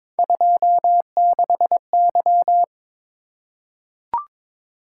Morse code, heard as a keyed tone, about 22 wpm, 700 Hz. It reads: 26Y
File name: practice / 22wpm / Three Letter-Number Combinations - 22wpm